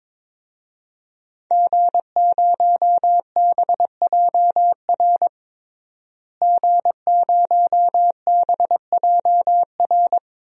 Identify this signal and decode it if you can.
Morse code, heard as a keyed tone, about 22 wpm, 700 Hz. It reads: G0BJR G0BJR
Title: DXCC-100 Call Signs - 5 Characters - Repeated 2x - 22wpm